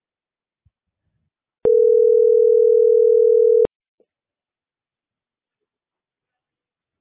Ringback tone (North America), 440 + 480 Hz.